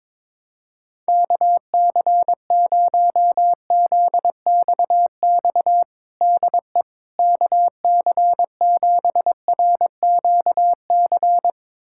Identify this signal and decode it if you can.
Morse code, heard as a keyed tone, about 22 wpm, 700 Hz. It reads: KC0ZXX DE KC7RQC
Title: USA Callsign DE Callsign 22wpm